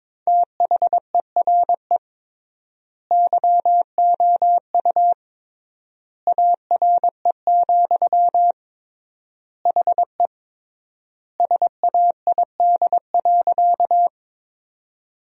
Morse code, heard as a keyed tone, about 21 wpm, 700 Hz. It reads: THERE YOU ARE, HE SAID.
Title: 3x Word-Spacing - Sentences from Top 100 Words - 22wpm